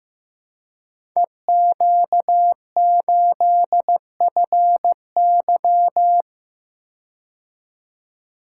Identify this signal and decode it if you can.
Morse code, heard as a keyed tone, about 15 wpm, 700 Hz. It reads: EQ8FY
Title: Code Groups - 5 Letter:Numbers - 15wpm